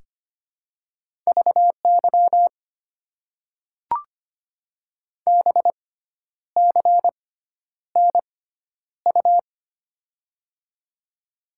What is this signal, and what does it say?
Morse code, heard as a keyed tone, about 25 wpm, spaced as for 10 wpm, 700 Hz. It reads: VY BCNU
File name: Top 90 QSO Elements - 6x Character Spacing - 25wpm